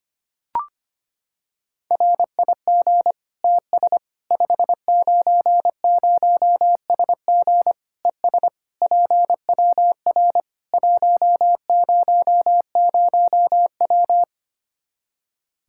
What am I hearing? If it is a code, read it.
Morse code, heard as a keyed tone, about 25 wpm, 700 Hz. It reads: RIG TS 590SG ES PWR 100W
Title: Practice QSO - Exchange 2 - Info - 25wpm